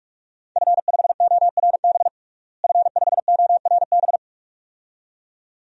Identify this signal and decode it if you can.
Morse code, heard as a keyed tone, about 46 wpm, 700 Hz. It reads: F5ORB F5ORB
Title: DXCC-100 Call Signs - 5 Characters - Repeated 2x - 45wpm